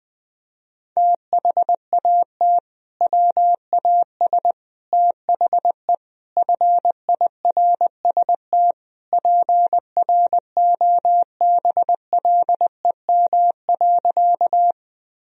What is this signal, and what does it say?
Morse code, heard as a keyed tone, about 20 wpm, 700 Hz. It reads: THAT WAS THE FIRST PROBLEM.
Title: Sentences from Top 1000 Words - Review Part 2 - 20wpm